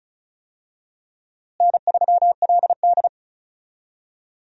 Morse code, heard as a keyed tone, about 36 wpm, 700 Hz. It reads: N3LD
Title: USA Call Signs - 4 Characters - Repeated 2x - 35wpm